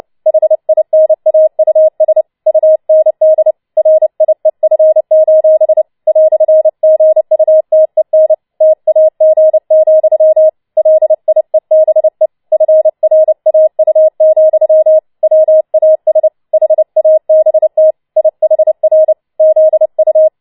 Morse code, heard as a keyed tone, about 29 wpm, 630 Hz. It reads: HINAUS UND RIEF: "GUTEN TAG, LIEBE FRAU, WAS HABT IHR ZU
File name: Schneewittchen Teil3 30wpm